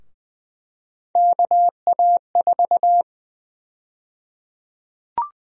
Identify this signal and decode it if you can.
Morse code, heard as a keyed tone, about 20 wpm, 700 Hz. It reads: KA4
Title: USA Call Sign Prefixes - Repeated 2x - 20wpm